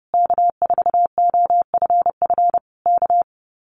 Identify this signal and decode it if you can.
Morse code, heard as a keyed tone, about 30 wpm, 700 Hz. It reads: K4OFF K
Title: Common QSO Phrases Encore 30wpm